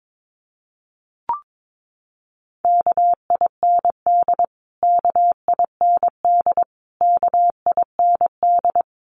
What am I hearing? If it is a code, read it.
Morse code, heard as a keyed tone, about 22 wpm, 700 Hz. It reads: KIND KIND KIND
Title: Top 500 Words Repeated 22wpm